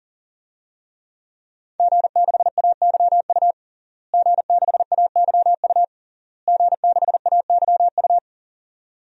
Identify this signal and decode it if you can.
Morse code, heard as a keyed tone, about 40 wpm, 700 Hz. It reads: G6AYU G6AYU G6AYU
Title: DXCC-100 Call Signs - 5 Characters - Repeated 3x - 40wpm